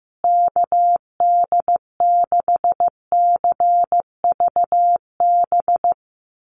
Morse code, heard as a keyed tone, about 15 wpm, 700 Hz. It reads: KD6CVB
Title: USA Call Signs Repeated 3x 15wpm